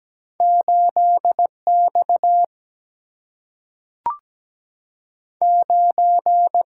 Morse code, heard as a keyed tone, about 17 wpm, 700 Hz. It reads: 8X 9
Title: Single-Letter Single-Number 17wpm